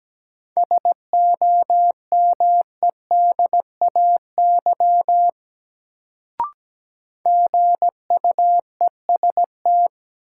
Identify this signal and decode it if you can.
Morse code, heard as a keyed tone, about 17 wpm, 700 Hz. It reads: SOMEDAY GUEST
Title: Sentences from Top 1600 Words - New Word Drill Alt - 17wpm